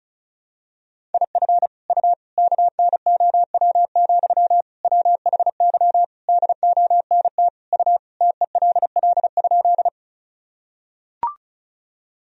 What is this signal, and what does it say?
Morse code, heard as a keyed tone, about 36 wpm, 700 Hz. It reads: IF U KNOW, WHY DONT U TELL?
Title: Sentences from Top 300 Words and Abbreviations 35wpm